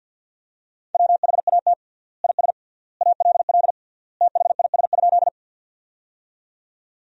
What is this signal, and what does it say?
Morse code, heard as a keyed tone, about 53 wpm, 700 Hz. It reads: WHAT IS ALL THIS?
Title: 3x Word-Spacing - Sentences from Top 100 Words - 50wpm